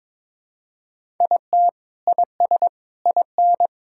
Morse code, heard as a keyed tone, about 22 wpm, 700 Hz. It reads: IT IS IN
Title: Sentences from Top 1000 Words - Review Part 6 - 22wpm